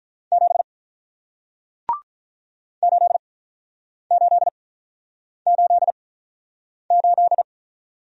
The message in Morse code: 7 8 8 8 8

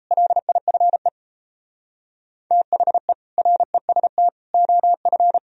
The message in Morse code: LIFE THE REST OF

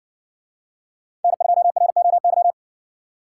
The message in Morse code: N2ROY

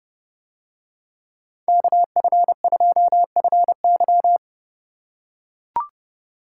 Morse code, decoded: KF2FY

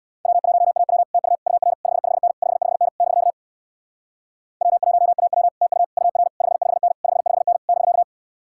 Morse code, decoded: W1AW TU UR 55N 55N <BK> W1AW TU UR 55N 55N <BK>

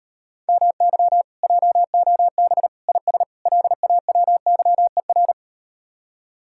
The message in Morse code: MY JOB IS LAWYER

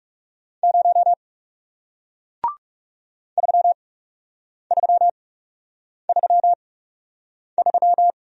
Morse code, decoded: 0 3333

